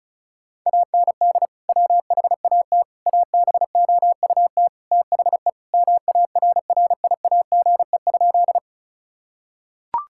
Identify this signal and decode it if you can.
Morse code, heard as a keyed tone, about 36 wpm, 700 Hz. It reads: AND WHAT ABOUT THE MARRIAGE?
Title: Sentences from Top 1000 Words - Review Part 5 - 35wpm